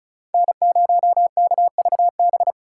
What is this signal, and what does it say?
Morse code, heard as a keyed tone, about 36 wpm, 700 Hz. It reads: N0KVB
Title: USA Callsign DE Callsign - 5 Characters - 35wpm